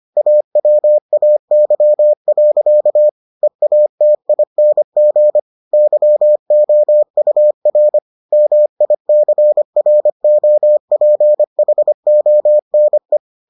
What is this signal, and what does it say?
Morse code, heard as a keyed tone, about 25 wpm, 600 Hz. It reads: AWAY. EATING YOUR MICROPHONE